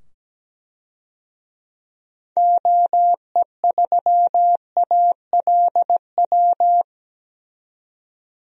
Morse code, heard as a keyed tone, about 17 wpm, 700 Hz. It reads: OE3ALW
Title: DXCC-100 Call Signs - 6 Characters - 17wpm